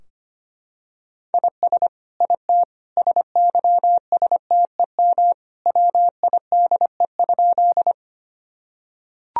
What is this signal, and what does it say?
Morse code, heard as a keyed tone, about 25 wpm, 700 Hz. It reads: IS IT SYSTEM WIDE?
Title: Sentences from Top 1000 Words - Review Part 3 - 25wpm